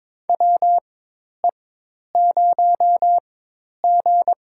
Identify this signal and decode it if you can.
Morse code, heard as a keyed tone, about 22 wpm, spaced as for 10 wpm, 700 Hz. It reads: WE0G
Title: USA Call Signs - 4 Characters - 4x Character Spacing - 22wpm